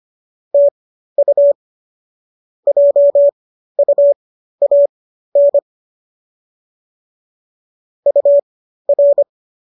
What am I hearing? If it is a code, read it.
Morse code, heard as a keyed tone, about 25 wpm, spaced as for 13 wpm, 570 Hz. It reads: TU JUAN UR